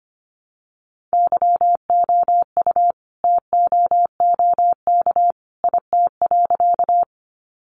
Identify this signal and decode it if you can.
Morse code, heard as a keyed tone, about 25 wpm, 700 Hz. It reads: YOU TOOK IT.
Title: Sentences from Top 300 Words 25wpm